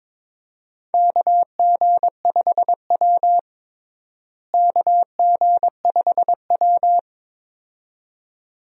Morse code, heard as a keyed tone, about 21 wpm, 700 Hz. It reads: KG5W KG5W
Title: USA Call Signs - 4 Characters - Repeated 2x - 22wpm